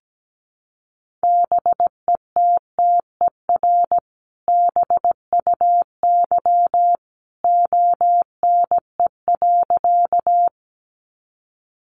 Morse code, heard as a keyed tone, about 17 wpm, 700 Hz. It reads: BETTER BUY ONE.
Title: Sentences from Top 700 Words 17wpm